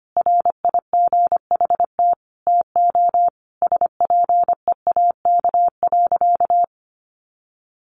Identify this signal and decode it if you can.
Morse code, heard as a keyed tone, about 25 wpm, 700 Hz. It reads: RIGHT TO SPEAK.
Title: Sentences from Top 1300 Words 25wpm